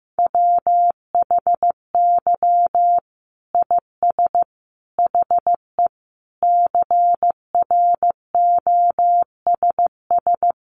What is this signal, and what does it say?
Morse code, heard as a keyed tone, about 15 wpm, 700 Hz. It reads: WHY IS HE CROSS
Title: Sentences from Top 300 Words 15wpm